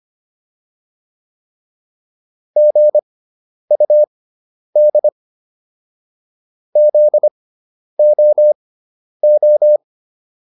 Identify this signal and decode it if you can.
Morse code, heard as a keyed tone, about 25 wpm, spaced as for 10 wpm, 600 Hz. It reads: GUD ZOO